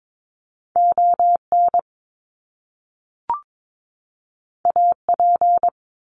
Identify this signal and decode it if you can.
Morse code, heard as a keyed tone, about 22 wpm, 700 Hz. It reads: ON AP
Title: Top Two Letter Combinations in Words 22wpm